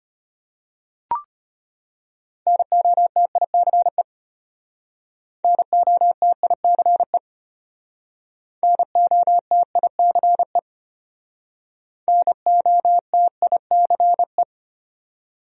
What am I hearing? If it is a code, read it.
Morse code, heard as a keyed tone, about 34 wpm, 700 Hz. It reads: NOTICE NOTICE NOTICE NOTICE